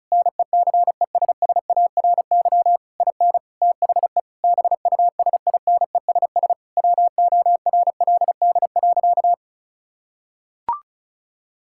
Morse code, heard as a keyed tone, about 36 wpm, 700 Hz. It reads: NECESSARY IN THE BUSINESS WORLD.
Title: Sentences from Top 1000 Words - Review Part 5 - 35wpm